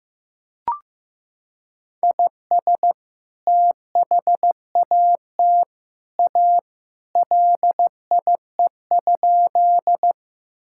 Morse code, heard as a keyed tone, about 15 wpm, 700 Hz. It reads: IS THAT A LIE?
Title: Sentences from Top 1600 Words 15wpm